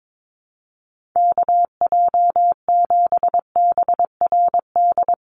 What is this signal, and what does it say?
Morse code, heard as a keyed tone, about 22 wpm, 700 Hz. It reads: KJ7BRD
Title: USA Call Signs 22wpm